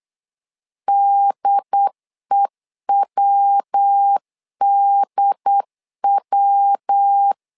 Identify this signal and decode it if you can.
Morse code, heard as a keyed tone, about 8 wpm, 800 Hz. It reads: DEWDW
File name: LEZIONI / 3 MWRO / 3 MW 8wpm 2501